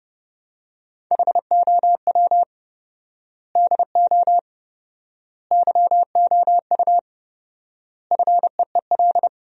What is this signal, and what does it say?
Morse code, heard as a keyed tone, about 30 wpm, 700 Hz. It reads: HOW DO YOU FEEL
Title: 4x Word-Spacing - Sentences from Top 400 Words - 30wpm